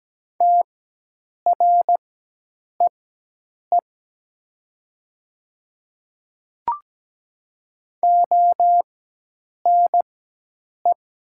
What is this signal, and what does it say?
Morse code, heard as a keyed tone, about 17 wpm, spaced as for 8 wpm, 700 Hz. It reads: TREE ONE
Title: 4x Character Spacing - Top 300 Words - 17wpm